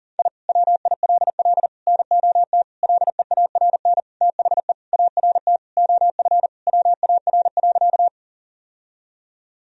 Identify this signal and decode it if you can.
Morse code, heard as a keyed tone, about 40 wpm, 700 Hz. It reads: I WILL NOT LEARN THE ART OF WAR.
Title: Sentences from Top 1000 Words - Review Part 3 - 40wpm